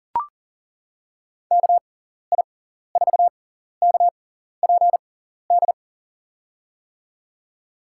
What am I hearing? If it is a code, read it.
Morse code, heard as a keyed tone, about 41 wpm, spaced as for 14 wpm, 700 Hz. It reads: KI4KPD
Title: USA Call Signs - 6 Characters - 6x Character Spacing - 40wpm